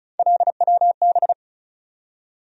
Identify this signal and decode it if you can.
Morse code, heard as a keyed tone, about 36 wpm, 700 Hz. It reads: LWB
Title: USA Call Sign Postfixes - 3 Letters 35wpm